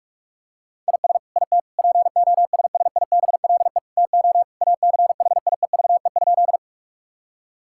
Morse code, heard as a keyed tone, about 46 wpm, 700 Hz. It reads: IS IT POSSIBLE TO ACHIEVE?
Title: Sentences from Top 1000 Words - Review Part 6 - 45wpm